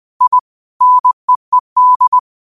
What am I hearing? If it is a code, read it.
Morse code, heard as a keyed tone, about 20 wpm, 1000 Hz. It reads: I NEED
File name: qotdfiles / 20wpm